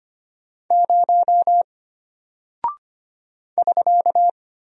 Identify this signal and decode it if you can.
Morse code, heard as a keyed tone, about 25 wpm, 700 Hz. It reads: 0 <SK>
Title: Single Character (Letter:Number:Punctuation:Prosign) Alt 25wpm